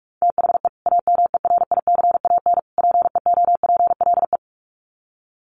Morse code, heard as a keyed tone, about 44 wpm, 700 Hz. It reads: THE AMERICAN PEOPLE
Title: Sets of 3 Words 45wpm